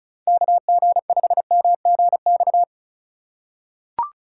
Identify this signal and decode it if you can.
Morse code, heard as a keyed tone, about 36 wpm, 700 Hz. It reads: KG5MGX
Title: USA Call Signs - 6 Characters - 3x Character Spacing - 35wpm